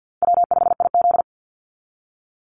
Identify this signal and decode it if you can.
Morse code, heard as a keyed tone, about 50 wpm, 700 Hz. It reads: W5IZ
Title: Call Signs - 50wpm